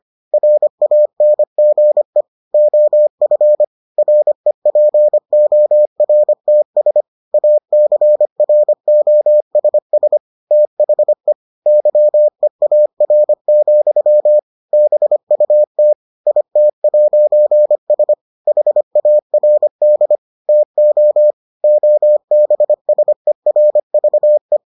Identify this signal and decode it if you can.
Morse code, heard as a keyed tone, about 25 wpm, 600 Hz. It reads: RANGE OF REPORTS ACROSS THE YEAR, BUT IT'S HARD TO OBSERVE